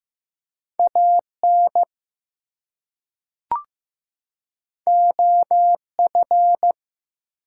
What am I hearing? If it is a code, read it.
Morse code, heard as a keyed tone, about 15 wpm, 700 Hz. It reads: AN OF
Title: Two Letter Words - 15wpm